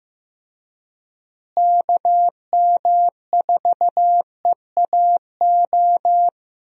Morse code, heard as a keyed tone, about 15 wpm, 700 Hz. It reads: KM4EAO
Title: USA Call Signs - 6 Characters - Repeated 3x - 15wpm